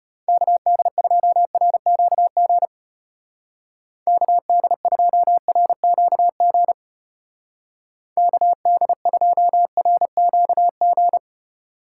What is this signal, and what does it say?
Morse code, heard as a keyed tone, about 32 wpm, 700 Hz. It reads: KD2RQG KD2RQG KD2RQG